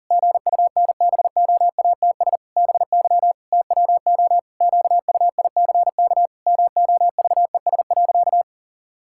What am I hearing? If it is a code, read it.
Morse code, heard as a keyed tone, about 37 wpm, 700 Hz. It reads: GUNBOATS BY TWO QUICK MOVES.
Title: Pangram Challenge v2 - 40wpm